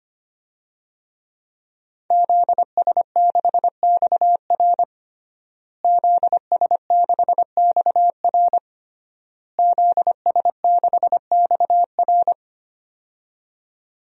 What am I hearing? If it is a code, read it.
Morse code, heard as a keyed tone, about 25 wpm, 700 Hz. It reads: ZS6XR ZS6XR ZS6XR
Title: DXCC-100 Call Signs - 5 Characters - Repeated 3x - 25wpm